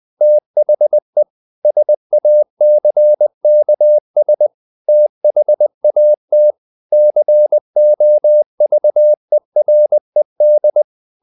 Morse code, heard as a keyed tone, about 20 wpm, 600 Hz. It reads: THE SACKS THAT COVERED
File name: the old man and the sea / the old man and the sea part 1 20wpm